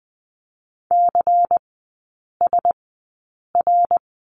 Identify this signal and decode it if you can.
Morse code, heard as a keyed tone, about 20 wpm, spaced as for 8 wpm, 700 Hz. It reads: CSR
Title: 138 Lesson 23e - Letter f - Three Characters 20wpm